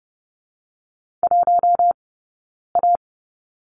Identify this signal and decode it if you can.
Morse code, heard as a keyed tone, about 30 wpm, 700 Hz. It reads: 1 A
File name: Lesson 17d - Letter u - Two Characters 30wpm